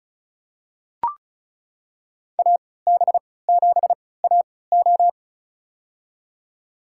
Morse code, heard as a keyed tone, about 36 wpm, spaced as for 20 wpm, 690 Hz. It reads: AB7AO
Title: USA Call Signs - 3x Character Spacing - 35wpm